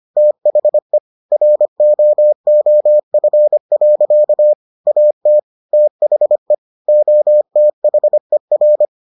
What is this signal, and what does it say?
Morse code, heard as a keyed tone, about 25 wpm, 600 Hz. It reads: THE ROOF. AT THE OTHER